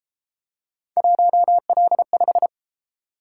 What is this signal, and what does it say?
Morse code, heard as a keyed tone, about 31 wpm, 700 Hz. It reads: 1L5